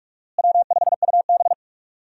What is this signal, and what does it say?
Morse code, heard as a keyed tone, about 44 wpm, 700 Hz. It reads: W5UB
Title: USA Call Signs - 4 Characters - Repeated 2x - 45wpm